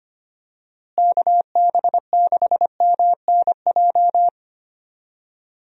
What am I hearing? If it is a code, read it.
Morse code, heard as a keyed tone, about 25 wpm, 700 Hz. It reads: KB6MNJ
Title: USA Call Signs - 6 Characters - Repeated 2x - 25wpm